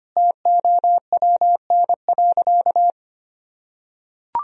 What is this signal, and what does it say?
Morse code, heard as a keyed tone, about 25 wpm, 700 Hz. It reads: TOWN.
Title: Sentences from Top 1000 Words - Review Part 2 - 25wpm